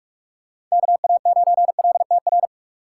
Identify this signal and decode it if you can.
Morse code, heard as a keyed tone, about 46 wpm, 700 Hz. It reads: KA9LTR
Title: USA Call Signs - 6 Characters - 4x Character Spacing - 45wpm